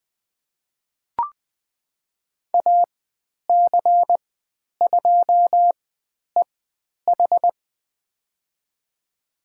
Morse code, heard as a keyed tone, about 20 wpm, spaced as for 10 wpm, 700 Hz. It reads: AC2EH
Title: 095 Lesson 16g - Letter c - Call Signs 20wpm